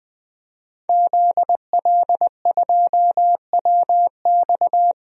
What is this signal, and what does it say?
Morse code, heard as a keyed tone, about 20 wpm, 700 Hz. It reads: ZL2WX
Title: DXCC-100 Call Signs - 5 Characters - Repeated 2x - 20wpm